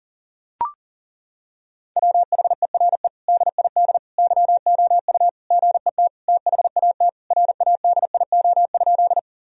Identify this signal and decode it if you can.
Morse code, heard as a keyed tone, about 40 wpm, 700 Hz. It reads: WHERE DID YOU GET THAT RADIO?
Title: Sentences from Top 1000 Words - Review Part 3 - 40wpm